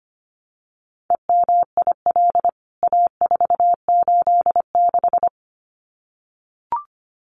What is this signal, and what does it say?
Morse code, heard as a keyed tone, about 25 wpm, 700 Hz. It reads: EMIL A486
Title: CWT Exchange 25wpm